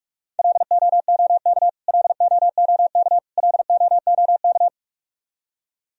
Morse code, heard as a keyed tone, about 45 wpm, 700 Hz. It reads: LOOK LOOK LOOK
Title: Top 100 Words Repeated 45wpm